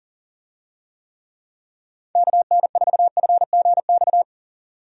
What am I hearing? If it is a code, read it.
Morse code, heard as a keyed tone, about 40 wpm, 700 Hz. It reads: KN4FGX
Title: USA Call Signs - 6 Characters - Repeated 2x - 40wpm